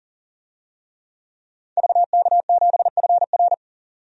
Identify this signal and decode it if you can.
Morse code, heard as a keyed tone, about 40 wpm, 700 Hz. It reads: VK7FR
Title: DXCC-100 Call Signs - 5 Characters - 40wpm